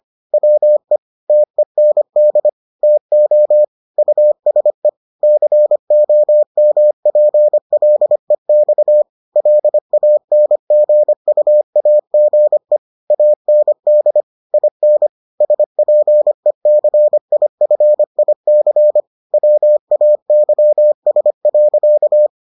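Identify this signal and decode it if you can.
Morse code, heard as a keyed tone, about 25 wpm, 600 Hz. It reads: WE TEND TO USE COMPLEX LANGUAGE AND IN SPECIFIC WAYS.
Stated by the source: Morse edition of the Foundations of Amateur Radio episodes.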